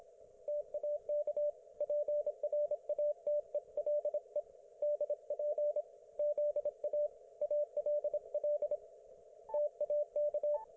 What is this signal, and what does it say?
Morse code, heard as a keyed tone, about 25 wpm, 590 Hz, with radio noise. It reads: TAK PRATELE DP ZA ALL TAK
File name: Archiv / CUC194 / ok1fhi cul194